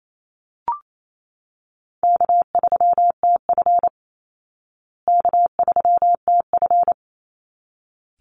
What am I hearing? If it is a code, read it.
Morse code, heard as a keyed tone, about 28 wpm, 700 Hz. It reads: K3TF K3TF
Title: Easy Call Signs Repeated 2x - 28wpm